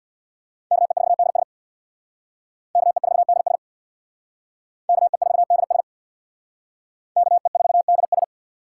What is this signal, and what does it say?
Morse code, heard as a keyed tone, about 63 wpm, 700 Hz. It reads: KE4DS KE4DS KE4DS KE4DS